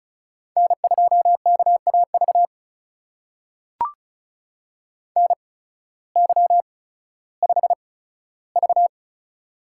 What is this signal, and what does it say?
Morse code, heard as a keyed tone, about 36 wpm, spaced as for 16 wpm, 700 Hz. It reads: N2KAV NY5V
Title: USA Call Signs - 8x Character Spacing - 35wpm